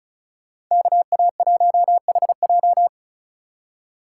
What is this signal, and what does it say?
Morse code, heard as a keyed tone, about 43 wpm, 700 Hz. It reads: KA1HJ
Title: Speed Racing - USA 2x2 Call Signs 35wpm